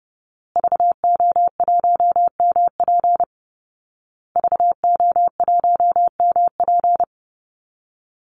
Morse code, heard as a keyed tone, about 30 wpm, 700 Hz. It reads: VO1MP VO1MP
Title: Call Signs Repeated 2x - 30wpm